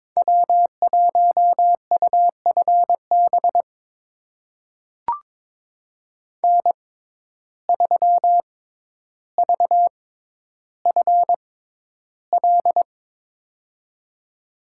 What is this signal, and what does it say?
Morse code, heard as a keyed tone, about 22 wpm, spaced as for 11 wpm, 700 Hz. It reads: W1UFB N3VFL
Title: USA Call Signs - 5 Characters - 6x Character Spacing - 22wpm